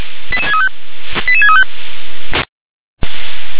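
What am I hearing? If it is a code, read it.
Five-tone selective call (ZVEI), heard as quick runs of five tones, tones of about 70 ms. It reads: 88635 88635
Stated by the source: Five-tone ZVEI selective call received by SDR.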